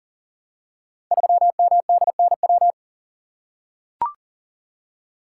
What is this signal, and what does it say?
Morse code, heard as a keyed tone, about 43 wpm, 700 Hz. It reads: 3MDNW
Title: Code Groups - 5 Letter:Numbers - 40wpm